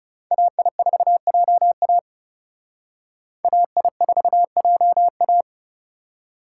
Morse code, heard as a keyed tone, about 38 wpm, 700 Hz. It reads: AI4JA AI4JA